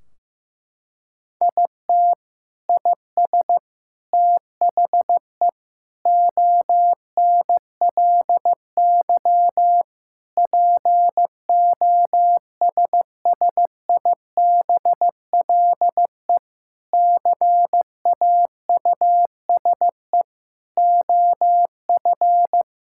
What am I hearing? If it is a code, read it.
Morse code, heard as a keyed tone, about 15 wpm, 700 Hz. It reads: IT IS THE ONLY POSSIBLE CAUSE OF
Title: Sentences from Top 1600 Words 15wpm